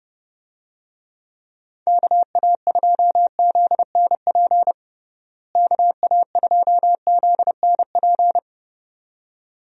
Morse code, heard as a keyed tone, about 29 wpm, 700 Hz. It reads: KA2ZNP KA2ZNP
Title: USA Call Signs - 6 Characters - Repeated 2x - 30wpm